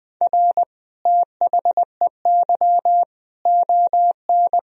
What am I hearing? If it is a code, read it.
Morse code, heard as a keyed tone, about 20 wpm, 700 Hz. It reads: R THEY ON
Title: Sentences from Top 500 Words and Abbreviations 20wpm